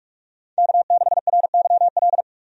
Morse code, heard as a keyed tone, about 43 wpm, 700 Hz. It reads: K6RYL
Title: USA Call Signs - 5 Characters - 45wpm